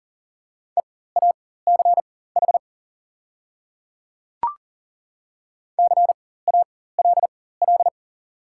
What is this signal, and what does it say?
Morse code, heard as a keyed tone, about 37 wpm, spaced as for 18 wpm, 690 Hz. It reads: EACH CALL
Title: 4x Character Spacing - Top 100 Words - 40wpm